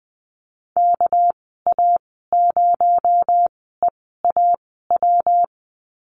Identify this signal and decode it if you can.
Morse code, heard as a keyed tone, about 19 wpm, spaced as for 14 wpm, 700 Hz. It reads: KA0EAW
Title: 219 Lesson 35g - Number 0 - Call Signs 20wpm